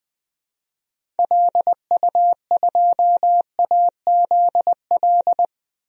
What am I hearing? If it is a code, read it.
Morse code, heard as a keyed tone, about 20 wpm, 700 Hz. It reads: LU2AZL
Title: DXCC-100 Call Signs - 6 Characters - 20wpm